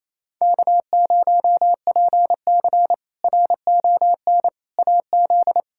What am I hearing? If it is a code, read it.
Morse code, heard as a keyed tone, about 28 wpm, 700 Hz. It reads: K0PC RON AZ
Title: K1USN SST QSO Practice - 28wpm